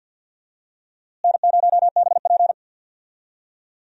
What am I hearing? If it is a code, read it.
Morse code, heard as a keyed tone, about 53 wpm, 700 Hz. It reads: N0BP
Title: USA Call Signs - 4 Characters - Repeated 3x - 50wpm